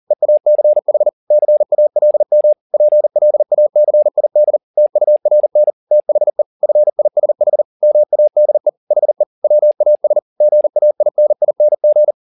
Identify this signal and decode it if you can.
Morse code, heard as a keyed tone, about 40 wpm, 600 Hz. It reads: EACH CALM PLACID TURN THE FISH MADE HE WAS GAINING
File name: the old man and the sea 2 40wpm